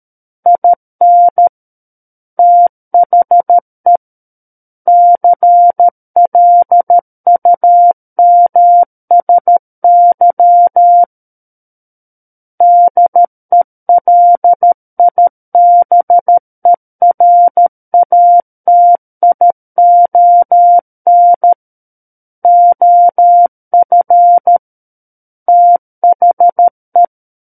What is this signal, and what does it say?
Morse code, heard as a keyed tone, about 13 wpm, 700 Hz. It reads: IN THE CLUMSY DELIBERATION OF THE
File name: Book One / 04-Chapter 4 - 13 WPM